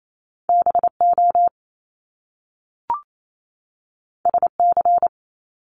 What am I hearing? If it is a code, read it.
Morse code, heard as a keyed tone, about 28 wpm, 700 Hz. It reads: BO SC
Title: Top Two Letter Combinations in Words 28wpm